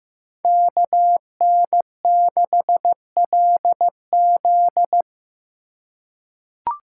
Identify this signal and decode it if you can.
Morse code, heard as a keyed tone, about 15 wpm, 700 Hz. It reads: KN6LZ
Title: USA Call Signs - 5 Characters - 3x Character Spacing - 15wpm